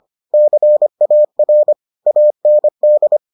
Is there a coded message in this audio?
Yes: Morse code from 0.3 to 3.2 s, heard as a keyed tone, about 25 wpm, 600 Hz. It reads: CAR AND